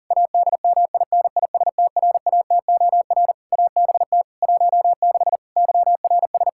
Morse code, heard as a keyed tone, about 40 wpm, 700 Hz. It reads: ADMINISTRATOR ABT 16 YRS
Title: Practice QSO - Exchange 3 - More Info - 40wpm